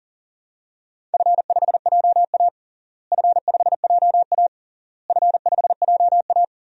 Morse code, heard as a keyed tone, about 40 wpm, 700 Hz. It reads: F5JA F5JA F5JA
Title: DXCC-100 Call Signs - 4 Characters - Repeated 3x - 40wpm